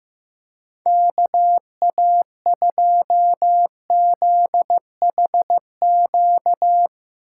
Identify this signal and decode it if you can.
Morse code, heard as a keyed tone, about 15 wpm, 700 Hz. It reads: KA2ZHQ
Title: USA Call Signs - 4x Character Spacing - 15wpm